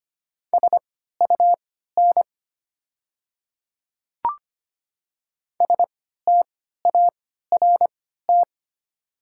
Morse code, heard as a keyed tone, about 25 wpm, spaced as for 14 wpm, 700 Hz. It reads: SUN START
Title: 3x Character Spacing - Top 500 Words - 25wpm